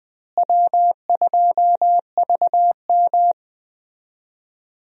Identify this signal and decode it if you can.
Morse code, heard as a keyed tone, about 20 wpm, 700 Hz. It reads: W2VM
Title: USA Call Signs - 4 Characters - Repeated 3x - 20wpm